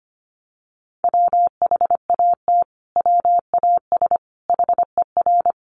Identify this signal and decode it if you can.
Morse code, heard as a keyed tone, about 25 wpm, 700 Hz. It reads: WHAT WAS HER
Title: Sentences from Top 1400 Words 25wpm